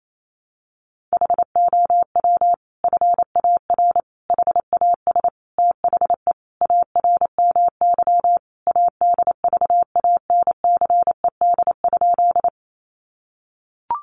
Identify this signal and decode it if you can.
Morse code, heard as a keyed tone, about 28 wpm, 700 Hz. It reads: HOW FAR HAS THE ARMY ADVANCED?
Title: Sentences from Top 1200 Words 28wpm